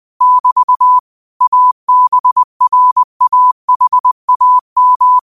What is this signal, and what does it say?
Morse code, heard as a keyed tone, about 20 wpm, 1000 Hz. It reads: <BT> ABRAHAM